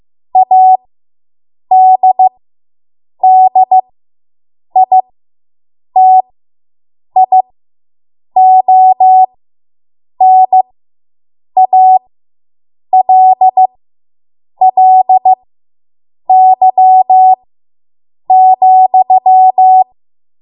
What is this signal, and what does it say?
Morse code, heard as a keyed tone, about 15 wpm, spaced as for 7 wpm, 750 Hz. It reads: ADDITIONALLY,